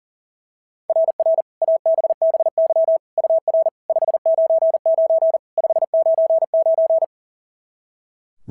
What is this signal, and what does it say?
Morse code, heard as a keyed tone, about 40 wpm, 660 Hz. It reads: RR ABBY UR 599 599
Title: SOTA QSO Practice 40wpm